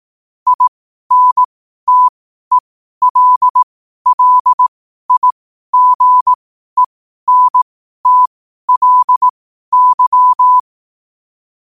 Morse code, heard as a keyed tone, about 18 wpm, spaced as for 12 wpm, 1000 Hz. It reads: INTELLIGENTLY
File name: qotdfiles / 12wpm